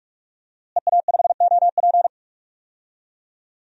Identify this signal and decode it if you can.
Morse code, heard as a keyed tone, about 46 wpm, 700 Hz. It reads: EA5OP
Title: DXCC-100 Call Signs - 5 Characters - Repeated 2x - 45wpm